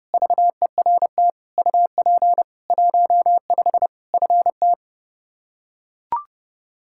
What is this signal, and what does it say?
Morse code, heard as a keyed tone, about 30 wpm, 700 Hz. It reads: VERT UP 15 FT
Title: Practice QSO - Exchange 2 - Info - 30wpm